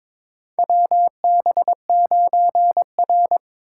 Morse code, heard as a keyed tone, about 22 wpm, 700 Hz. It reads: WB9R
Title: USA Call Signs - 4 Characters - 3x Character Spacing - 22wpm